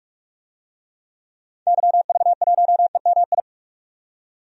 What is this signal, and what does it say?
Morse code, heard as a keyed tone, about 46 wpm, 700 Hz. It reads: YV1EMI
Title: DXCC-100 Call Signs - 6 Characters - 45wpm